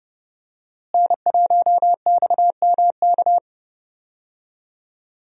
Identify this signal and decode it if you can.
Morse code, heard as a keyed tone, about 31 wpm, 700 Hz. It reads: N1XMK
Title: USA Call Signs - 5 Characters - Repeated 2x - 30wpm